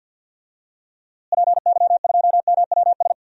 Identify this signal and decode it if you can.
Morse code, heard as a keyed tone, about 49 wpm, 700 Hz. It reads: PY2MWS
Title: DXCC-100 Call Signs - 6 Characters - Repeated 2x - 50wpm